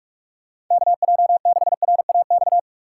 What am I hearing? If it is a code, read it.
Morse code, heard as a keyed tone, about 46 wpm, 700 Hz. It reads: KJ6RAX
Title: USA Call Signs - 2x Character Spacing - 45wpm